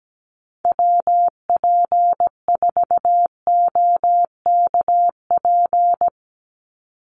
Morse code, heard as a keyed tone, about 17 wpm, 700 Hz. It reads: WP4OKP
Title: USA Call Signs 17wpm